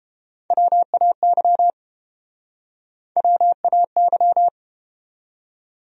Morse code, heard as a keyed tone, about 29 wpm, 700 Hz. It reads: WAY WAY